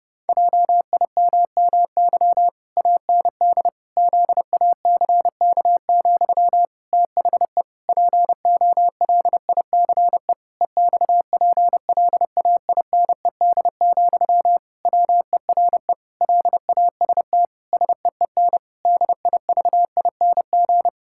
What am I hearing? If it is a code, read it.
Morse code, heard as a keyed tone, about 30 wpm, 700 Hz. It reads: JIMMY AND ZACK, THE POLICE EXPLAINED, WERE LAST SEEN DIVING
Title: Pangram Challenge v2 - 30wpm